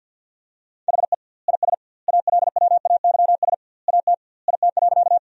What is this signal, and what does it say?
Morse code, heard as a keyed tone, about 52 wpm, 700 Hz. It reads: HE IS ALWAYS AT IT.
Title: 2x Word-Spacing - Sentences from Top 300 Words - 50wpm